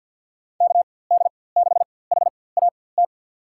Morse code, heard as a keyed tone, about 50 wpm, spaced as for 23 wpm, 700 Hz. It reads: KD6HAT
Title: USA Call Signs - 4x Character Spacing - 50wpm